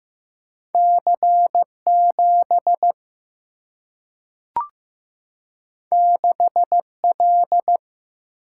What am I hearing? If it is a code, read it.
Morse code, heard as a keyed tone, about 15 wpm, 700 Hz. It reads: C7 6L
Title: Single-Letter Single-Number 15wpm